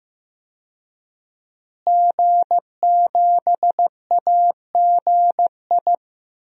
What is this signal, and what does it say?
Morse code, heard as a keyed tone, about 15 wpm, 700 Hz. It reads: G7AGI
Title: DXCC-100 Call Signs - 5 Characters - Repeated 3x - 15wpm